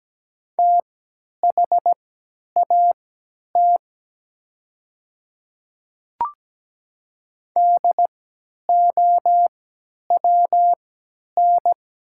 Morse code, heard as a keyed tone, about 17 wpm, spaced as for 10 wpm, 700 Hz. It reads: THAT DOWN
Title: 3x Character Spacing - Top 100 Words - 17wpm